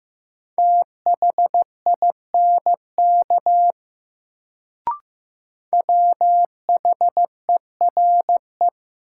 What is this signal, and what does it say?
Morse code, heard as a keyed tone, about 16 wpm, 700 Hz. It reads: THINK WHERE
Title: Sentences from Top 200 Words - New Word Drill - 15wpm